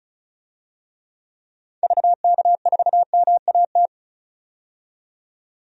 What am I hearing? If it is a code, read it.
Morse code, heard as a keyed tone, about 34 wpm, 700 Hz. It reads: VK4MAT
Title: DXCC-100 Call Signs - 6 Characters - 35wpm